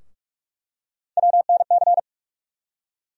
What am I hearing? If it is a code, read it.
Morse code, heard as a keyed tone, about 46 wpm, 700 Hz. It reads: WNC
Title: USA Call Sign Postfixes - 3 Letters 45wpm